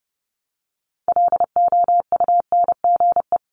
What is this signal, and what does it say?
Morse code, heard as a keyed tone, about 30 wpm, 700 Hz. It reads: LOUNGE
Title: Multi-letter Phonemes Repeated 3x 30wpm